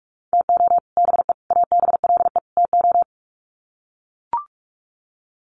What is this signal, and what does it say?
Morse code, heard as a keyed tone, about 44 wpm, 700 Hz. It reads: TO BE ABLE TO
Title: Sets of 4 Words 45wpm